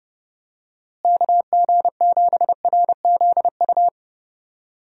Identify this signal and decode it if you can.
Morse code, heard as a keyed tone, about 30 wpm, 700 Hz. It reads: KG7RZU
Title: USA Call Signs - 6 Characters - Repeated 3x - 30wpm